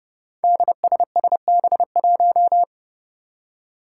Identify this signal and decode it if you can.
Morse code, heard as a keyed tone, about 32 wpm, 700 Hz. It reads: DSSB1
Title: Five Challenging Letter-Number Combinations 30wpm